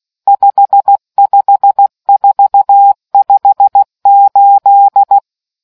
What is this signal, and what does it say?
Morse code, heard as a keyed tone, about 16 wpm, 800 Hz. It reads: 55458